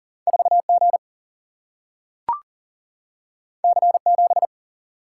Morse code, heard as a keyed tone, about 40 wpm, 700 Hz. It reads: 4 G C 7
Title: Single-Letter Single-Number 40wpm